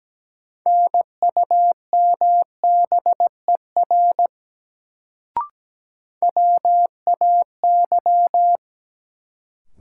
Morse code, heard as a keyed tone, about 17 wpm, 700 Hz. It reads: NUMBER WAY
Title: Sentences from Top 100 Words - New Word Drill - 17wpm